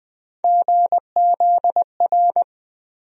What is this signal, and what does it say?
Morse code, heard as a keyed tone, about 20 wpm, 700 Hz. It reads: GZR
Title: USA Call Sign Postfixes - 3 Letters 20wpm